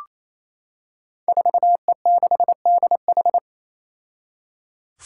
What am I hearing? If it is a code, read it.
Morse code, heard as a keyed tone, about 28 wpm, 700 Hz. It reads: 4E6DH
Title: Five Challenging Letter-Number Combinations 28wpm